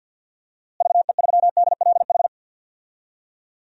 Morse code, heard as a keyed tone, about 53 wpm, 700 Hz. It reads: VE3DLH
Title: DXCC-100 Call Signs - 6 Characters - Repeated 2x - 50wpm